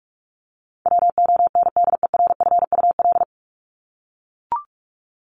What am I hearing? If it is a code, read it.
Morse code, heard as a keyed tone, about 44 wpm, 700 Hz. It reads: WONDERFUL
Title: Top 1001-1250 Words 45wpm